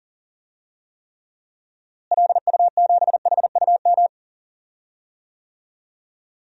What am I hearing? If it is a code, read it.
Morse code, heard as a keyed tone, about 40 wpm, 700 Hz. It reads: LU7HUM
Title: DXCC-100 Call Signs - 6 Characters - Repeated 2x - 40wpm